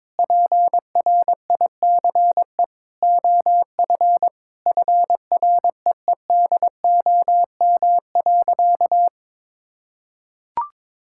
Morse code, heard as a keyed tone, about 22 wpm, 700 Hz. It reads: PRICE OF FREEDOM.
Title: Sentences from Top 1000 Words - Review Part 5 - 22wpm